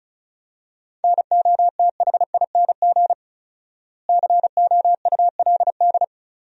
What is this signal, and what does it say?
Morse code, heard as a keyed tone, about 34 wpm, 700 Hz. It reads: NOTHING COULD
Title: 4x Word-Spacing - Sentences from Top 500 Words - 35wpm